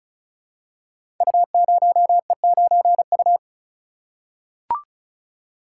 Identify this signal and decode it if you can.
Morse code, heard as a keyed tone, about 36 wpm, 700 Hz. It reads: U0E9U
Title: Code Groups - 5 Letter:Numbers - 35wpm